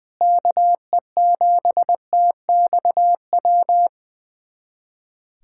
Morse code, heard as a keyed tone, about 21 wpm, 700 Hz. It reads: KE7TXW
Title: USA Callsign DE Callsign 20wpm